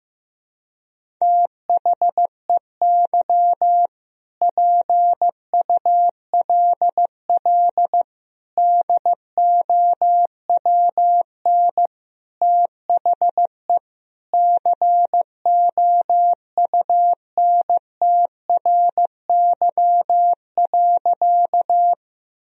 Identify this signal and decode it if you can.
Morse code, heard as a keyed tone, about 15 wpm, 700 Hz. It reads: THEY PULL DOWN THE COUNTRY.
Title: Sentences from Top 1000 Words - Review Part 2 - 15wpm